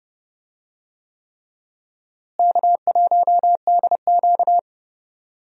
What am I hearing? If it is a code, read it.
Morse code, heard as a keyed tone, about 30 wpm, 700 Hz. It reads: K1DQ
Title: USA Call Signs - 4 Characters - Repeated 3x - 30wpm